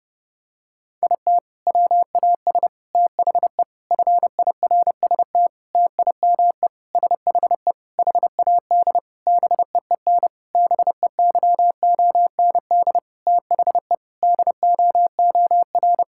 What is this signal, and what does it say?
Morse code, heard as a keyed tone, about 30 wpm, 700 Hz. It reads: IT WAS THE FIRST TIME SHE HAD BEEN BEYOND THE DOOR
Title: Sentences from Top 1000 Words - Review Part 6 - 30wpm